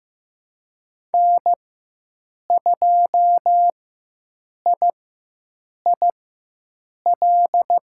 Morse code, heard as a keyed tone, about 16 wpm, spaced as for 7 wpm, 700 Hz. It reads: N2IIL
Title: USA Call Signs - 5 Characters - 4x Character Spacing - 15wpm